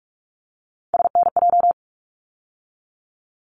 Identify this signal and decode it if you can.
Morse code, heard as a keyed tone, about 44 wpm, 700 Hz. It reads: SNJ
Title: ARRL Sections 45wpm